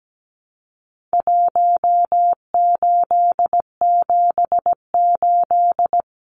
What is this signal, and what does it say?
Morse code, heard as a keyed tone, about 17 wpm, 700 Hz. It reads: 1878
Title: Numbers - Contest Style - 4-Digit 17wpm